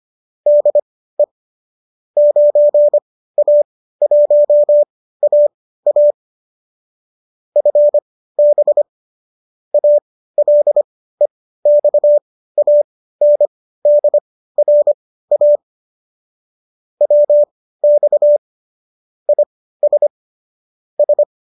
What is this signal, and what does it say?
Morse code, heard as a keyed tone, about 25 wpm, spaced as for 15 wpm, 590 Hz. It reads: DE 9A1AA FB ALEXANDRA WX IS S
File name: qso219 15